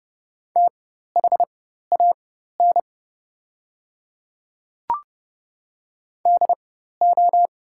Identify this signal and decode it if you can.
Morse code, heard as a keyed tone, about 29 wpm, spaced as for 14 wpm, 700 Hz. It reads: THAN DO
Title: 4x Character Spacing - Top 100 Words - 30wpm